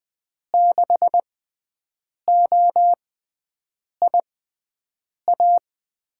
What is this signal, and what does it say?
Morse code, heard as a keyed tone, about 20 wpm, spaced as for 7 wpm, 700 Hz. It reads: 6OIA
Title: USA Call Signs - 6 Characters - 6x Character Spacing - 20wpm